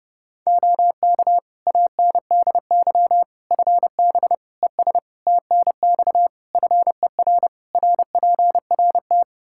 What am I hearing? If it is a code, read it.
Morse code, heard as a keyed tone, about 32 wpm, 700 Hz. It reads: OK ANDY FB ES TNX FER RPRT
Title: K1USN SST QSO Practice 30wpm